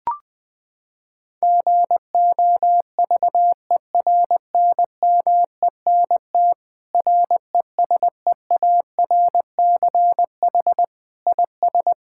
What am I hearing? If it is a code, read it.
Morse code, heard as a keyed tone, about 21 wpm, 700 Hz. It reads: GOVERNMENT RESEARCH IS
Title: Sentences from Top 1000 Words - Review Part 6 - 20wpm